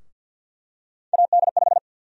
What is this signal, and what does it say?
Morse code, heard as a keyed tone, about 51 wpm, 700 Hz. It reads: AD5
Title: USA Call Sign Prefixes - Repeated 2x - 50wpm